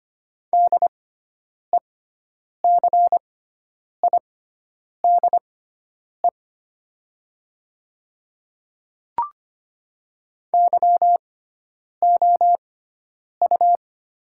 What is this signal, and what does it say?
Morse code, heard as a keyed tone, about 25 wpm, spaced as for 9 wpm, 700 Hz. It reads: DECIDE YOU
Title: 6x Character Spacing - Top 500 Words - 25wpm